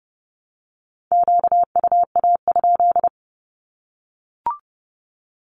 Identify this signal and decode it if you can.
Morse code, heard as a keyed tone, about 30 wpm, 700 Hz. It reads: QUA?
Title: Q-Codes 30wpm